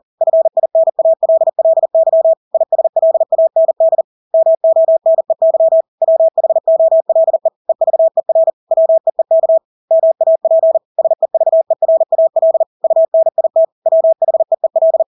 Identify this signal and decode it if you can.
Morse code, heard as a keyed tone, about 40 wpm, 650 Hz. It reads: FINALLY ISLAND MONEY WHOLE EVER WEEK MAP SEVERAL UNIT WHEEL
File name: Engelske ord 40-40 wpm